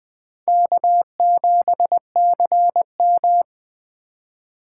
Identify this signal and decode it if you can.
Morse code, heard as a keyed tone, about 24 wpm, 700 Hz. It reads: K7CM
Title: Speed Racing - USA 1x2 Call Signs 20wpm